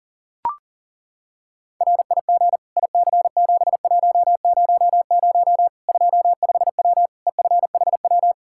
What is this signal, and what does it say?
Morse code, heard as a keyed tone, about 37 wpm, 700 Hz. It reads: RIG IC7100 25W EFHW
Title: Practice QSO - Exchange 2 - Info - 40wpm